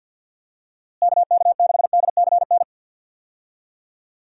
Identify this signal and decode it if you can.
Morse code, heard as a keyed tone, about 52 wpm, 700 Hz. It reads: KK6DCN
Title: USA Call Signs - 6 Characters - Repeated 3x - 50wpm